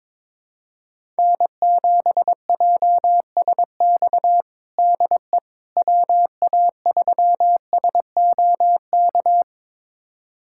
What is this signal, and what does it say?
Morse code, heard as a keyed tone, about 22 wpm, 700 Hz. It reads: N7JSX DE WA3SOK
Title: USA Callsign DE Callsign 22wpm